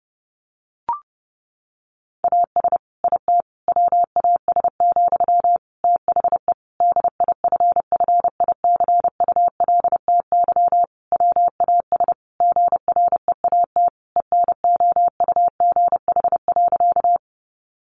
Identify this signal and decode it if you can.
Morse code, heard as a keyed tone, about 28 wpm, 700 Hz. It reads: AS IT WAS, THE DIFFICULTY WAS GREAT ENOUGH.
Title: Sentences from Top 1500 Words 30wpm